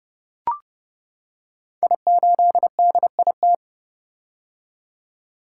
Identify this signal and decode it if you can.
Morse code, heard as a keyed tone, about 28 wpm, 700 Hz. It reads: I8DIT
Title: Code Groups - 5 Letter:Numbers - 30wpm